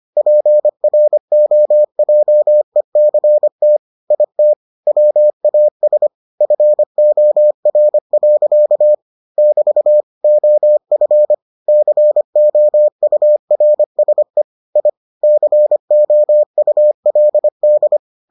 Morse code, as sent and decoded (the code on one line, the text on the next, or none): .--. .-. --- .--- . -.-. - / .. - / .-- .- ... / ..-. --- .-. .-.-.- / -...- / --- ..-. / -.-. --- ..- .-. ... / .. / -.-. --- ..- .-.. -..
PROJECT IT WAS FOR. <BT> OF COURSE I COULD